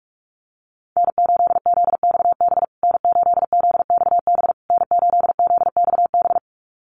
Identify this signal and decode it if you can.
Morse code, heard as a keyed tone, about 44 wpm, 700 Hz. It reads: N8ZXB N8ZXB N8ZXB
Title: USA Call Signs Repeated 3x 45wpm